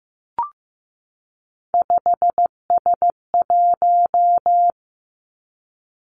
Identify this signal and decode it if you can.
Morse code, heard as a keyed tone, about 15 wpm, 700 Hz. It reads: 5S1
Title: Three Challenging Letter-Number Combinations 15wpm